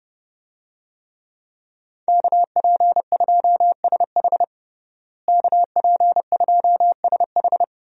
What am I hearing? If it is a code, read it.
Morse code, heard as a keyed tone, about 30 wpm, 700 Hz. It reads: KP2SH KP2SH
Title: DXCC-100 Call Signs - 5 Characters - Repeated 2x - 30wpm